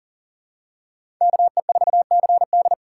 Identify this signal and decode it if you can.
Morse code, heard as a keyed tone, about 42 wpm, 690 Hz. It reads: KE4CD
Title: USA Call Signs - 5 Characters - 3x Character Spacing - 40wpm